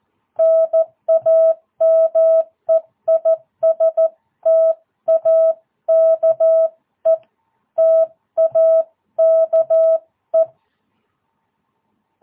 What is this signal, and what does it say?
Morse code, heard as a keyed tone, about 14 wpm, 660 Hz. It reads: NAME IS TAKE TAKE
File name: jh8cbh / 2022cw